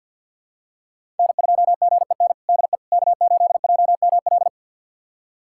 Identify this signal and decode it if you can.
Morse code, heard as a keyed tone, about 47 wpm, 700 Hz. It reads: N2GEN DE K8JML
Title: USA Callsign DE Callsign - 5 Characters - 50wpm